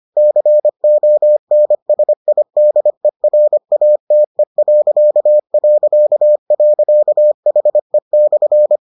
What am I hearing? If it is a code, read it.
Morse code, heard as a keyed tone, about 25 wpm, 600 Hz. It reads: CONSIDERATE...HE/